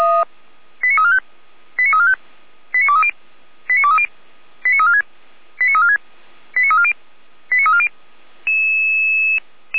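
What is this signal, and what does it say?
Five-tone selective call (ZVEI), heard as quick runs of five tones, tones of about 70 ms. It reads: 89345 89345 89230 89230 89346 89346 89340 89340
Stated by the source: ZVEI five-tone selective call received on SDR.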